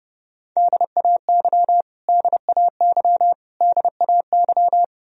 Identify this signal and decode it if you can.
Morse code, heard as a keyed tone, about 29 wpm, 700 Hz. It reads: DAY DAY DAY
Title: Top 100 Words Repeated 30wpm